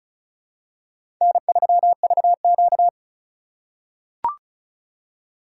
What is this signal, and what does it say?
Morse code, heard as a keyed tone, about 36 wpm, 700 Hz. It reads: N3VQ
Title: USA Call Signs - 4 Characters - 2x Character Spacing - 35wpm